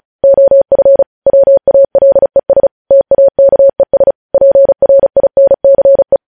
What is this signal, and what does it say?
Morse code, heard as a keyed tone, about 36 wpm, 550 Hz. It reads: OF WALES TAKES PRINCE
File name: podcast / News headlines-1x-35x35